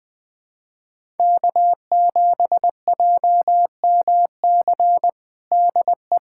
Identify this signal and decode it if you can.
Morse code, heard as a keyed tone, about 21 wpm, 700 Hz. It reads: K7JMC DE
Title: USA Callsign DE Callsign 20wpm